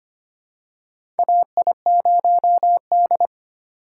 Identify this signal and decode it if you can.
Morse code, heard as a keyed tone, about 25 wpm, 700 Hz. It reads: AI0D
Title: USA Call Signs - 4 Characters - 4x Character Spacing - 25wpm